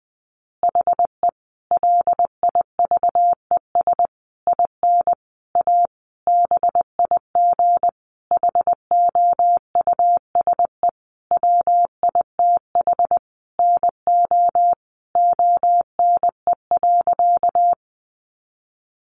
Morse code, heard as a keyed tone, about 19 wpm, 700 Hz. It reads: HE LIVES IN A BIG HOUSE WITH NO ONE.
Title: Additional Easy Phrases 20wpm